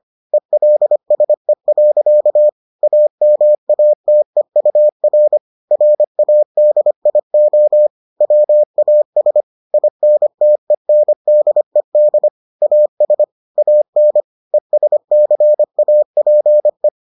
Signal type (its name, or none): Morse code